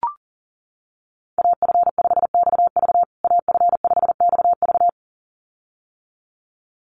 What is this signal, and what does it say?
Morse code, heard as a keyed tone, about 40 wpm, 700 Hz. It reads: AF5XV AF5XV
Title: USA Call Signs Repeated 2x 40wpm